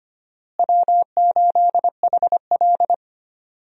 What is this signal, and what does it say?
Morse code, heard as a keyed tone, about 25 wpm, 700 Hz. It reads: W8HL
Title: USA Call Signs - 4 Characters - Repeated 2x - 25wpm